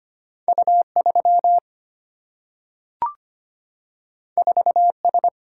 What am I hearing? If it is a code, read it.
Morse code, heard as a keyed tone, about 25 wpm, 700 Hz. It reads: U3 4S
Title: Single-Letter Single-Number 25wpm